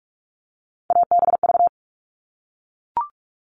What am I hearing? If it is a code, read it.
Morse code, heard as a keyed tone, about 44 wpm, 700 Hz. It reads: ABV
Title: All Cut Numbers - 3-Digit 45wpm